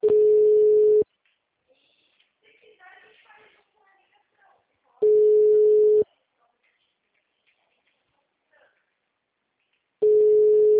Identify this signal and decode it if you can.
Ringback tone (Europe, 425 Hz), with speech.